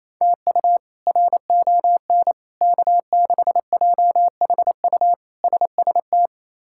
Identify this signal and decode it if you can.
Morse code, heard as a keyed tone, about 27 wpm, 700 Hz. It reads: TU RON K6JHU SST
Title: K1USN SST QSO Practice - 28wpm